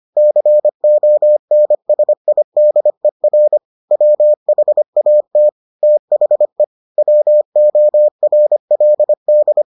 Morse code, heard as a keyed tone, about 25 wpm, 600 Hz. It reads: CONSIDER WHAT THE WORLD